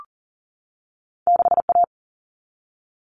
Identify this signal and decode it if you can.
Morse code, heard as a keyed tone, about 40 wpm, 700 Hz. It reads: 6A
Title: All Cut Numbers - 2-Digit 40wpm